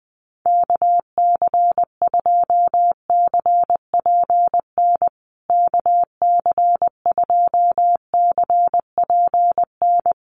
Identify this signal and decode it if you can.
Morse code, heard as a keyed tone, about 20 wpm, 700 Hz. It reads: KC2CPN KC2CPN
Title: USA Call Signs Repeated 3x 20wpm